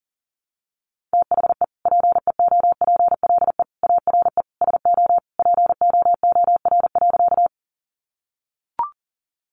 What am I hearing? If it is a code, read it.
Morse code, heard as a keyed tone, about 40 wpm, 700 Hz. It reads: THE PEOPLE ARE SO POOR.
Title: Sentences from Top 1000 Words 40wpm